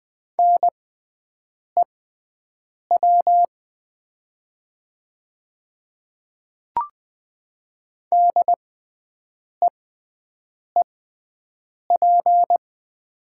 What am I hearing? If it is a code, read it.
Morse code, heard as a keyed tone, about 20 wpm, spaced as for 7 wpm, 700 Hz. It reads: NEW DEEP
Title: 6x Character Spacing - Top 500 Words - 20wpm